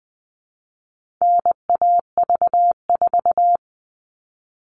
Morse code, heard as a keyed tone, about 20 wpm, 700 Hz. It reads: NAV4
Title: All Cut Numbers - 4-Digit 20wpm